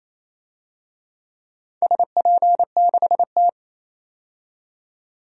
Morse code, heard as a keyed tone, about 28 wpm, 700 Hz. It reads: SP6T
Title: DXCC-100 Call Signs - 4 Characters - 28wpm